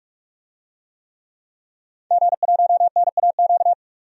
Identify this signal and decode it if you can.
Morse code, heard as a keyed tone, about 44 wpm, 700 Hz. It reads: G1NAQ
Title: DXCC-100 Call Signs - 5 Characters - 45wpm